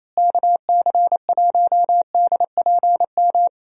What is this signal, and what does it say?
Morse code, heard as a keyed tone, about 28 wpm, 700 Hz. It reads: KC1DPM
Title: USA Call Signs - 6 Characters - Repeated 3x - 28wpm